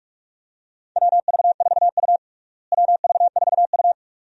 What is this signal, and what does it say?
Morse code, heard as a keyed tone, about 46 wpm, 700 Hz. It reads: WV4U WV4U
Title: USA Call Signs - 4 Characters - Repeated 2x - 45wpm